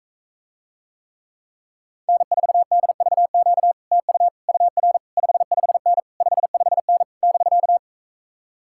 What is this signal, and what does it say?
Morse code, heard as a keyed tone, about 44 wpm, 700 Hz. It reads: N4DVQ TU UR 55N 55N <BK>